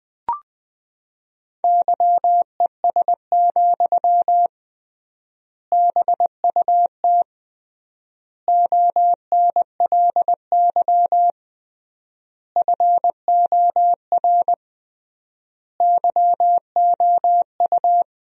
Morse code, heard as a keyed tone, about 20 wpm, 700 Hz. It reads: YES, BUT ONLY FOR YOU
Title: 3x Word-Spacing - Sentences from Top 500 Words - 20wpm